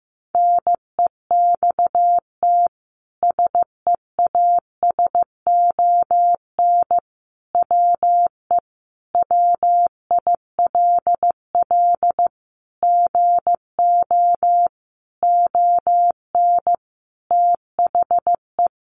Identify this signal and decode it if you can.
Morse code, heard as a keyed tone, about 15 wpm, 700 Hz. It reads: NEXT SEASON WE WILL GO ON THE
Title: Sentences from Top 1400 Words 15wpm